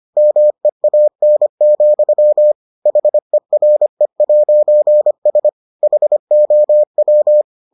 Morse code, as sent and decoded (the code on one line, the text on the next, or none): -- . .- -. --..-- / .... .-. .----. ... / .... --- .--
MEAN, HERE'S HOW